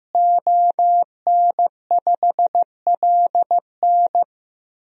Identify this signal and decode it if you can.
Morse code, heard as a keyed tone, about 15 wpm, 700 Hz. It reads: ON5LN
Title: DXCC-100 Call Signs - 5 Characters - Repeated 2x - 15wpm